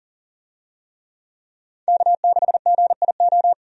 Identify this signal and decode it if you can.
Morse code, heard as a keyed tone, about 40 wpm, 700 Hz. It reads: K6GIO
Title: USA Call Signs - 5 Characters - 40wpm